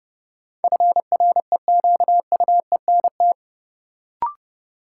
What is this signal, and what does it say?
Morse code, heard as a keyed tone, about 30 wpm, 700 Hz. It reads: FREQUENT
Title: Sentences from Top 1400 Words - New Word Drill - 30wpm